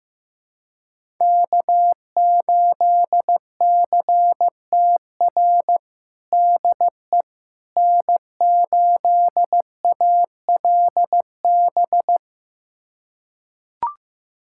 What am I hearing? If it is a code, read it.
Morse code, heard as a keyed tone, about 15 wpm, 700 Hz. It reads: K8CTR DE N8ALB
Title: USA Callsign DE Callsign - 5 Characters - 15wpm